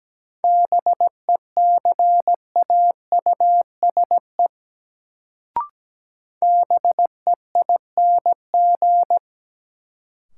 Morse code, heard as a keyed tone, about 17 wpm, 700 Hz. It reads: BECAUSE BEING
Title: Sentences from Top 600 Words - New Word Drill - 17wpm